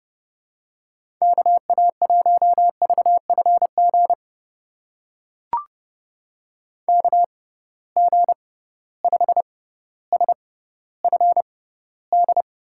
Morse code, heard as a keyed tone, about 30 wpm, spaced as for 15 wpm, 700 Hz. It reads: KA1VFG KG5SFD
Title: USA Call Signs - 6 Characters - 6x Character Spacing - 30wpm